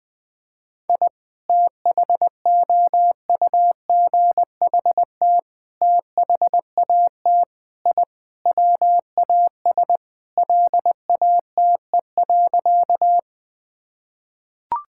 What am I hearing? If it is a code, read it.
Morse code, heard as a keyed tone, about 20 wpm, 700 Hz. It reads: I THOUGHT THAT I WAS LATE.
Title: Sentences with 3-Word Phrases - Spoken Beginning - 20wpm